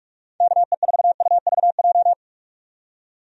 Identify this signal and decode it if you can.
Morse code, heard as a keyed tone, about 47 wpm, 700 Hz. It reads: KE4UVJ
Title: USA Call Signs - 6 Characters - 45wpm